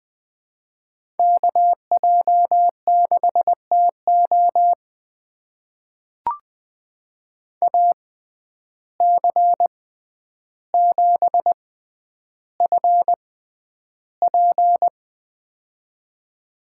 Morse code, heard as a keyed tone, about 20 wpm, spaced as for 10 wpm, 700 Hz. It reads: KJ6TO AC7FP
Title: USA Call Signs - 5 Characters - 6x Character Spacing - 20wpm